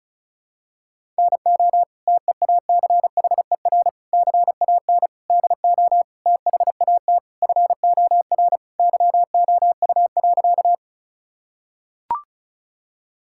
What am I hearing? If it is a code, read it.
Morse code, heard as a keyed tone, about 36 wpm, 700 Hz. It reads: NO TEACHER CAN DO THAT FOR YOU.
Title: Sentences from Top 1000 Words - Review Part 6 - 35wpm